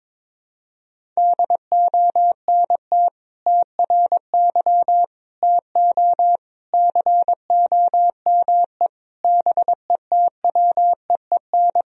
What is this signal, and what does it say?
Morse code, heard as a keyed tone, about 22 wpm, 700 Hz. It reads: DONT TRY TO COME BETWEEN
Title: Sentences from Top 1000 Words - Review Part 1 - 22wpm